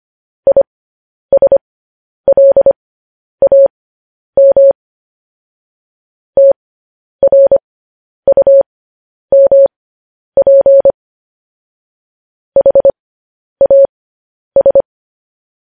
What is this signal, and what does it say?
Morse code, heard as a keyed tone, about 25 wpm, spaced as for 10 wpm, 550 Hz. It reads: ISLAM TRUMP HAS
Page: news headlines in morse code at 10 wpm